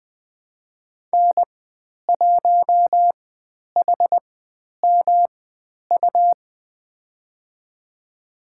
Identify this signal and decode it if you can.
Morse code, heard as a keyed tone, about 20 wpm, spaced as for 10 wpm, 700 Hz. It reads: N1HMU
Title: 109 Lesson 18g - Letter m - Call Signs 20wpm